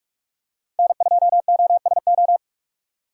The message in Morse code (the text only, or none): N2OSO